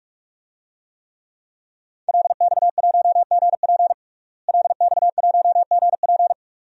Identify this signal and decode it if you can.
Morse code, heard as a keyed tone, about 46 wpm, 700 Hz. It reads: LX1GP LX1GP
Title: DXCC-100 Call Signs - 5 Characters - Repeated 2x - 45wpm